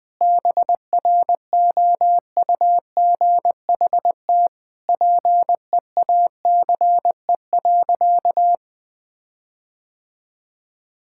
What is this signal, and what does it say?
Morse code, heard as a keyed tone, about 20 wpm, 700 Hz. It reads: BROUGHT PEACE.
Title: Sentences with 5-Word Phrases - 20wpm